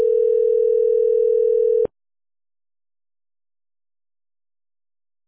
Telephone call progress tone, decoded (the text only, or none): ringback tone